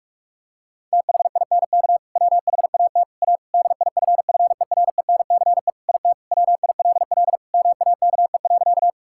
Morse code, heard as a keyed tone, about 41 wpm, 700 Hz. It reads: THINK WHAT A DIFFERENCE IT WILL MAKE.
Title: Sentences from Top 1000 Words - Review Part 5 - 45wpm